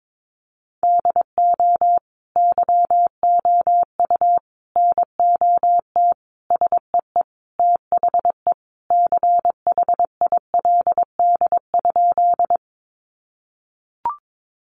Morse code, heard as a keyed tone, about 22 wpm, 700 Hz. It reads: DO YOU NOT SEE THE CHILD?
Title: Sentences from Top 800 Words 22wpm